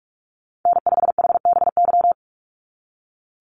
Morse code, heard as a keyed tone, about 44 wpm, 700 Hz. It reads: N5HBY
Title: USA Call Signs 45wpm